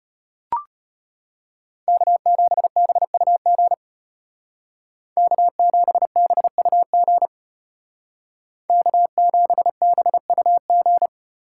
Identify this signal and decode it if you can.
Morse code, heard as a keyed tone, about 31 wpm, 700 Hz. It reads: K7BUG K7BUG K7BUG
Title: Speed Racing - USA 1x3 Call Signs 25wpm